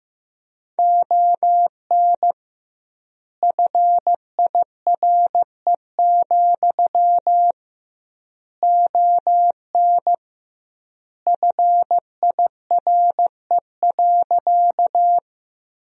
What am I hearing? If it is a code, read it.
Morse code, heard as a keyed tone, about 15 wpm, 700 Hz. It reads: ON FIRE, ON FIRE.
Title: 2x Word-Spacing - Sentences from Top 400 Words - 15wpm